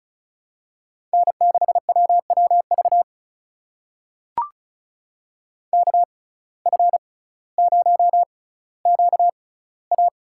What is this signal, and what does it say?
Morse code, heard as a keyed tone, about 36 wpm, spaced as for 18 wpm, 700 Hz. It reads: N6WWV KF0QA
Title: USA Call Signs - 5 Characters - 6x Character Spacing - 35wpm